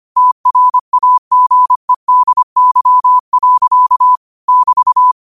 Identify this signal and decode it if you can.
Morse code, heard as a keyed tone, about 25 wpm, 1000 Hz. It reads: TRAGEDY. <BT>